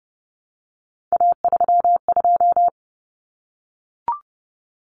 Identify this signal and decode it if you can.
Morse code, heard as a keyed tone, about 30 wpm, 700 Hz. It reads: A32
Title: Common Cut Numbers - 3-Digit 30wpm